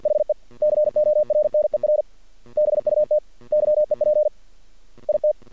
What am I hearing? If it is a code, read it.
Morse code, heard as a keyed tone, about 48 wpm, 600 Hz. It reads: HE CORNER BUT YEP IT
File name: archived / CRSnet